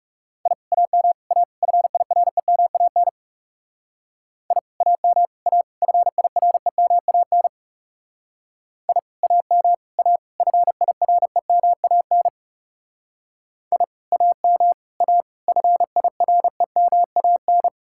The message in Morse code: I AM A FIREMAN I AM A FIREMAN I AM A FIREMAN I AM A FIREMAN

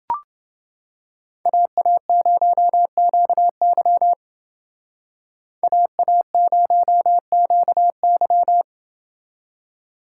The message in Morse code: AA0QY AA0QY

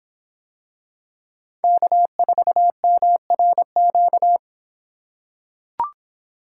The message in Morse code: K4MRQ

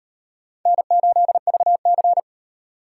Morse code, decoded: N8VC

